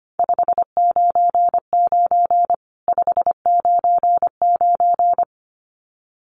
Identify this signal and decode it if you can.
Morse code, heard as a keyed tone, about 25 wpm, 700 Hz. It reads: 599 599
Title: Common QSO Phrases 25wpm